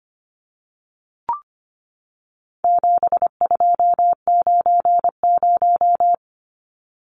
Morse code, heard as a keyed tone, about 25 wpm, 700 Hz. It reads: 7290
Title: Numbers - 4-Digit 25wpm